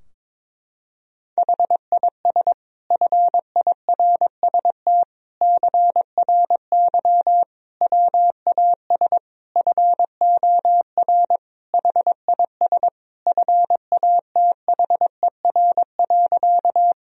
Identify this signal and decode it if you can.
Morse code, heard as a keyed tone, about 22 wpm, 700 Hz. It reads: HIS FIRST CRY WAS FOR HIS FATHER.
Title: Sentences from Top 1000 Words - Review Part 2 - 22wpm